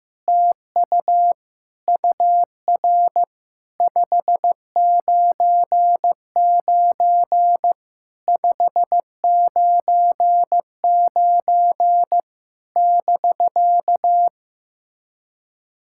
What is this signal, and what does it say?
Morse code, heard as a keyed tone, about 15 wpm, 700 Hz. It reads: TU UR 599 599 <BK>
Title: SOTA QSO Practice 15wpm